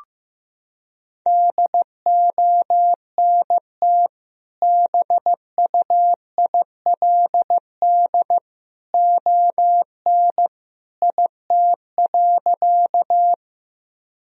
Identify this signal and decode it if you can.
Morse code, heard as a keyed tone, about 15 wpm, 700 Hz. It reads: DONT BUILD ON IT.
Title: Sentences from Top 1000 Words - Review Part 1 - 15wpm